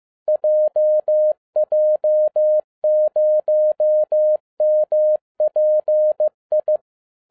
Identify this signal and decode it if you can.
Morse code, heard as a keyed tone, about 15 wpm, 610 Hz. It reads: JJ0MPI
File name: 2012-01-mp3 / n13-A NEWLY DESIGNED QSL CARD-15wpm